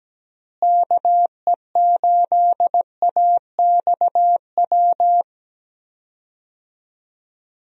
Morse code, heard as a keyed tone, about 17 wpm, 700 Hz. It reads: KE8AXW
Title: USA Call Signs - 6 Characters - Repeated 3x - 17wpm